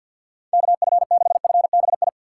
Morse code, heard as a keyed tone, about 52 wpm, 700 Hz. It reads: KF6FBI
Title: USA Call Signs - 6 Characters - 3x Character Spacing - 50wpm